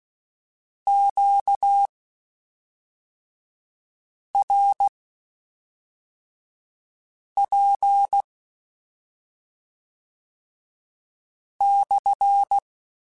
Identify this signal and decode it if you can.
Morse code, heard as a keyed tone, about 16 wpm, spaced as for 3 wpm, 800 Hz. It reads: QRP /
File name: Q Calls / Q Calls-mp3s / Q Calls SPSL2 @5